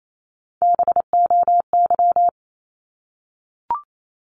Morse code, heard as a keyed tone, about 28 wpm, 700 Hz. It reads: BOY
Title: Top 200 Words 28wpm